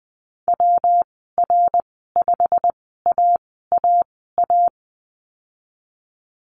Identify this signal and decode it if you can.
Morse code, heard as a keyed tone, about 20 wpm, spaced as for 14 wpm, 700 Hz. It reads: WR5AAA
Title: 193 Lesson 31g - Letter v - Call Signs 20wpm